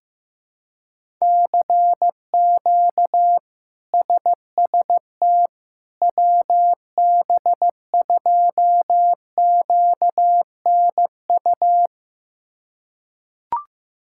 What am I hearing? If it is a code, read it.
Morse code, heard as a keyed tone, about 15 wpm, 700 Hz. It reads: CQ SST WB2QNU
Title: K1USN SST QSO Practice - 15wpm